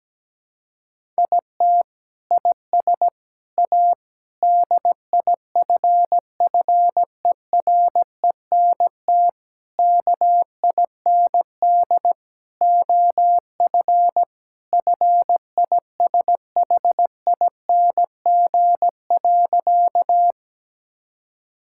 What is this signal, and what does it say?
Morse code, heard as a keyed tone, about 17 wpm, 700 Hz. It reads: IT IS A DIFFERENT KIND OF FISHING.
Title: Sentences from Top 1000 Words - Review Part 5 - 17wpm